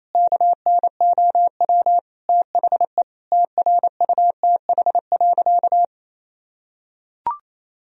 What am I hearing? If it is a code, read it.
Morse code, heard as a keyed tone, about 28 wpm, 700 Hz. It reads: KNOW THE TRUTH.
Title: Sentences from Top 1000 Words - Review Part 3 - 28wpm